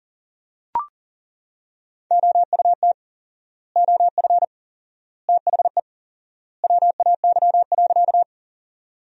Morse code, heard as a keyed tone, about 37 wpm, 700 Hz. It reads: OUT OF THE WAY.
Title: 4x Word-Spacing - Sentences from Top 100 Words - 40wpm